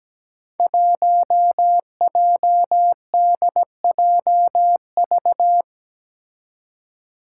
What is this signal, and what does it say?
Morse code, heard as a keyed tone, about 17 wpm, 700 Hz. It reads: 1JDJV
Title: Five Challenging Letter-Number Combinations 17wpm